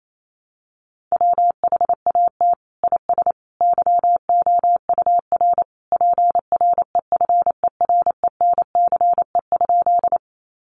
Morse code, heard as a keyed tone, about 28 wpm, 700 Hz. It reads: WHAT IS YOUR PREFERENCE?
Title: Sentences from Top 1500 Words 28wpm